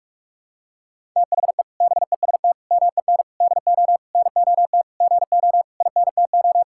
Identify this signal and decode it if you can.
Morse code, heard as a keyed tone, about 48 wpm, 700 Hz. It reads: THE BEST MEN DO NOT GO INTO
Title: Sentences from Top 1000 Words - Review Part 6 - 45wpm